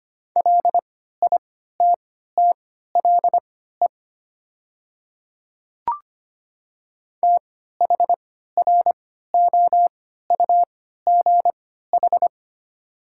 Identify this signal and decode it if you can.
Morse code, heard as a keyed tone, about 25 wpm, spaced as for 14 wpm, 700 Hz. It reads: LITTLE THROUGH
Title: 3x Character Spacing - Top 500 Words - 25wpm